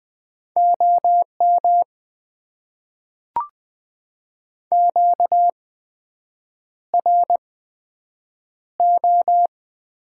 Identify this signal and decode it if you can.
Morse code, heard as a keyed tone, about 20 wpm, spaced as for 7 wpm, 700 Hz. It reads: OM QRO
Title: Top 90 QSO Elements - 8x Character Spacing - 20wpm